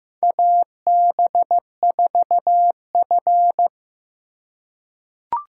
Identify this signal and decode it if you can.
Morse code, heard as a keyed tone, about 15 wpm, 700 Hz. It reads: AB4F
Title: USA Call Signs - 4 Characters - 3x Character Spacing - 15wpm